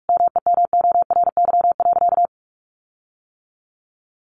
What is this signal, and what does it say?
Morse code, heard as a keyed tone, about 44 wpm, 700 Hz. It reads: MEMORY.
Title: Sentences from Top 1300 Words 45wpm